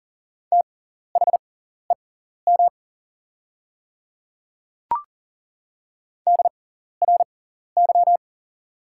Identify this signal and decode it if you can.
Morse code, heard as a keyed tone, about 38 wpm, spaced as for 14 wpm, 690 Hz. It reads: THEM DRY
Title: 6x Character Spacing - Top 500 Words - 40wpm